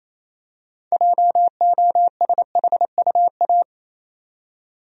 Morse code, heard as a keyed tone, about 28 wpm, 700 Hz. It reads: JOSHUA
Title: USA Common Names - 28wpm